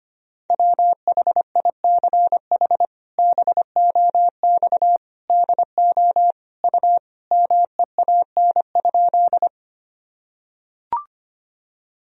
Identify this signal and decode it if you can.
Morse code, heard as a keyed tone, about 25 wpm, 700 Hz. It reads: WHICH BOX DO U MEAN?
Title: Sentences from Top 500 Words and Abbreviations 25wpm